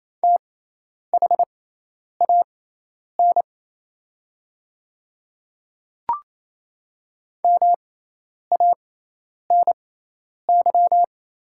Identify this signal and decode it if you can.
Morse code, heard as a keyed tone, about 28 wpm, spaced as for 10 wpm, 700 Hz. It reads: THAN MANY
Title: 6x Character Spacing - Top 200 Words - 28wpm